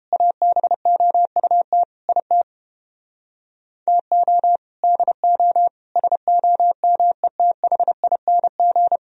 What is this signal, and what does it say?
Morse code, heard as a keyed tone, about 29 wpm, 700 Hz. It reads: ABOUT IT TO DO SOMETHING